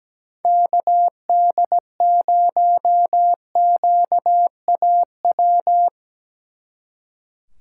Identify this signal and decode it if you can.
Morse code, heard as a keyed tone, about 19 wpm, 700 Hz. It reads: KD0QAW
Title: Speed Racing - USA Call Signs 17wpm